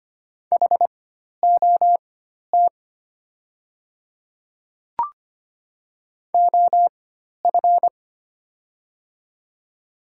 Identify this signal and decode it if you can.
Morse code, heard as a keyed tone, about 25 wpm, spaced as for 12 wpm, 700 Hz. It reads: HOT OF
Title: 4x Character Spacing - Top 100 Words - 25wpm